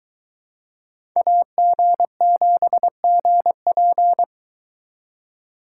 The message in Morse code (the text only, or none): AG7GP